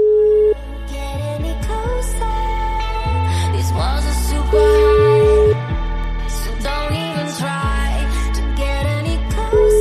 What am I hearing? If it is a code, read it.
Ringback tone (Europe, 425 Hz).